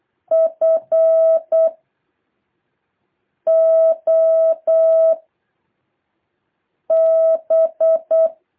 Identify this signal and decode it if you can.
Morse code, heard as a keyed tone, about 8 wpm, 650 Hz. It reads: F O B
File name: jh8cbh / 2023cw